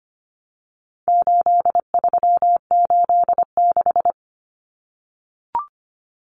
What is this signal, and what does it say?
Morse code, heard as a keyed tone, about 25 wpm, 700 Hz. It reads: 8386
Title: Numbers - 4-Digit 25wpm